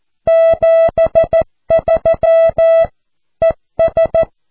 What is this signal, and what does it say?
Morse code, heard as a keyed tone, about 14 wpm, 660 Hz. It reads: 73 ES
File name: cw / QSO